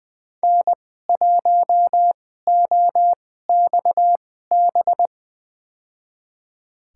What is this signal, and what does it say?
Morse code, heard as a keyed tone, about 20 wpm, spaced as for 14 wpm, 700 Hz. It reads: N1OXB
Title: USA Call Signs - 5 Characters - 2x Character Spacing - 20wpm